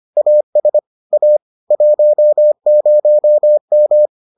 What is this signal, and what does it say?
Morse code, heard as a keyed tone, about 25 wpm, 600 Hz. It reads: AS A 10M